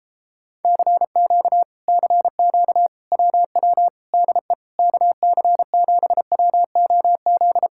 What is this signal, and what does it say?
Morse code, heard as a keyed tone, about 33 wpm, 700 Hz. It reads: CQ CQ WW DE KC7WOZ